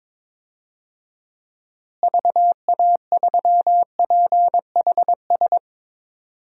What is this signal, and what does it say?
Morse code, heard as a keyed tone, about 22 wpm, 700 Hz. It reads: VA3PHS
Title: DXCC-100 Call Signs - 6 Characters - Repeated 3x - 22wpm